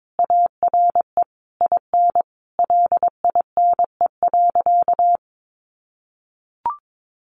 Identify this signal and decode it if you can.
Morse code, heard as a keyed tone, about 22 wpm, 700 Hz. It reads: ARE IN LINE.
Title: Additional Easy Phrases 22wpm